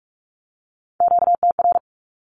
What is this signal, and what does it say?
Morse code, heard as a keyed tone, about 44 wpm, 700 Hz. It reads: QTR
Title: Q-Codes 45wpm